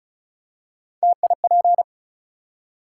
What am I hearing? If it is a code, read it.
Morse code, heard as a keyed tone, about 36 wpm, 700 Hz. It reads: TIP
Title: Three Letter Words - 35wpm